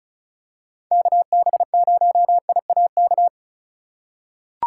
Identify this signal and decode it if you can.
Morse code, heard as a keyed tone, about 36 wpm, 700 Hz. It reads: KB0IAK
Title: USA Call Signs - 6 Characters - 4x Character Spacing - 35wpm